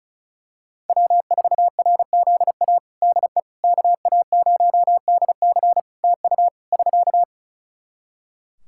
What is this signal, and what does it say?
Morse code, heard as a keyed tone, about 36 wpm, 700 Hz. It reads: W4RZA DE KA0DC TU <SK>
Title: Practice QSO - Exchange 4 - Ending - 35wpm